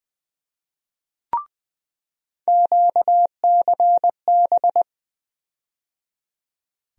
Morse code, heard as a keyed tone, about 20 wpm, 700 Hz. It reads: QCB
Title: USA Call Sign Postfixes - 3 Letters 20wpm